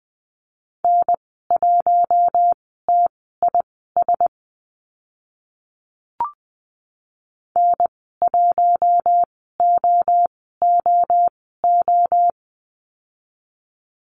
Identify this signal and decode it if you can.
Morse code, heard as a keyed tone, about 20 wpm, spaced as for 14 wpm, 700 Hz. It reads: N1TIS N1OOO
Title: 042 Lesson 8g - Number 1 - Call Signs 20wpm